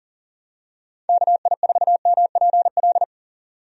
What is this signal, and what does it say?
Morse code, heard as a keyed tone, about 40 wpm, 700 Hz. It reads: KI4MPL
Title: USA Call Signs - 6 Characters - 40wpm